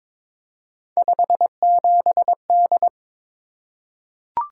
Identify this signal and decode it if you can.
Morse code, heard as a keyed tone, about 22 wpm, 700 Hz. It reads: 57D
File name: practice / 22wpm / Three Letter-Number Combinations - 22wpm